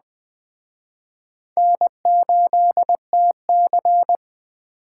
Morse code, heard as a keyed tone, about 20 wpm, 700 Hz. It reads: N8TC
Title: Speed Racing - USA 1x2 Call Signs 17wpm